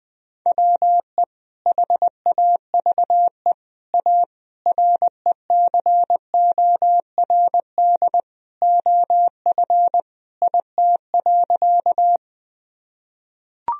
Morse code, heard as a keyed tone, about 20 wpm, 700 Hz. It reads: WE HAVE A RECORD OF IT.
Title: Sentences from Top 1000 Words - Review Part 3 - 20wpm